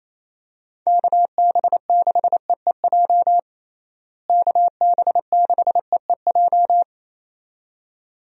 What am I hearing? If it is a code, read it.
Morse code, heard as a keyed tone, about 27 wpm, 700 Hz. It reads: KB6EEJ KB6EEJ
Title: USA Call Signs - 6 Characters - Repeated 2x - 28wpm